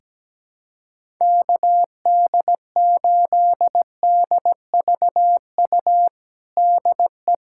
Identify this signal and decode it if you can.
Morse code, heard as a keyed tone, about 17 wpm, 700 Hz. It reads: KD8DVU DE
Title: USA Callsign DE Callsign 17wpm